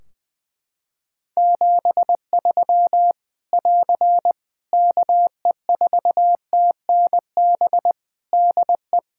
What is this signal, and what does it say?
Morse code, heard as a keyed tone, about 20 wpm, 700 Hz. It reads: 73 <AR> KE4TNB DE
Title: Practice QSO - Exchange 4 - Ending - 20wpm